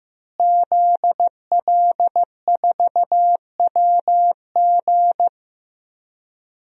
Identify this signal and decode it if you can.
Morse code, heard as a keyed tone, about 15 wpm, 700 Hz. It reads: ZL4WG
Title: DXCC-100 Call Signs - 5 Characters - Repeated 3x - 15wpm